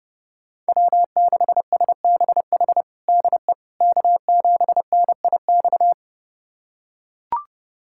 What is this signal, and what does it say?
Morse code, heard as a keyed tone, about 30 wpm, 700 Hz. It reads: W6SBH DE K7NIX
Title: USA Callsign DE Callsign 30wpm